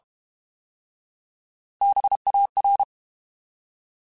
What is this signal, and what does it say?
Morse code, heard as a keyed tone, about 32 wpm, 800 Hz. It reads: BAR
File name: Alphabetical / Alphabetical mp3s / Bs @30